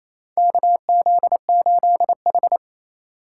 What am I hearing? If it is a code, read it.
Morse code, heard as a keyed tone, about 28 wpm, 700 Hz. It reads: KZ8H
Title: USA Call Signs - 4x Character Spacing - 28wpm